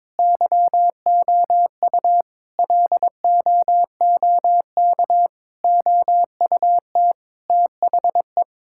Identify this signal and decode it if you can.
Morse code, heard as a keyed tone, about 22 wpm, 700 Hz. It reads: YOU LOOK OUT THE
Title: Sentences from Top 1600 Words 22wpm